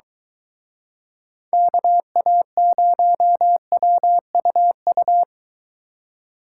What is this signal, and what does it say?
Morse code, heard as a keyed tone, about 23 wpm, 700 Hz. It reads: KA0WUU